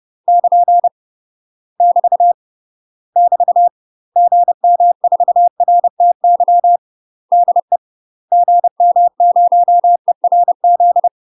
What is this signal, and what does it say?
Morse code, heard as a keyed tone, about 30 wpm, 700 Hz. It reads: <KN> <BT> <BT> GM4RTY DE GM0ERZ
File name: UK MASTER QSOs TRAINING / 30WPM / UK MASTER QSOs 0004